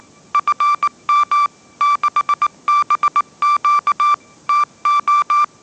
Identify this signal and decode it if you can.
Morse code, heard as a keyed tone, about 19 wpm, 1170 Hz. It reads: FM 6BQ TO
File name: mrx083 20wpm